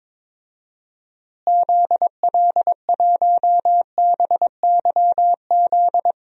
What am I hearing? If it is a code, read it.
Morse code, heard as a keyed tone, about 21 wpm, 700 Hz. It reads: ZL1BYZ
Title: DXCC-100 Call Signs - 6 Characters - Repeated 2x - 22wpm